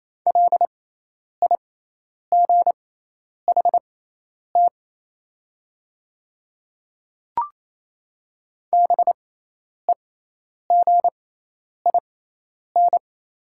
Morse code, heard as a keyed tone, about 28 wpm, spaced as for 10 wpm, 700 Hz. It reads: LIGHT BEGIN
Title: 6x Character Spacing - Top 300 Words - 28wpm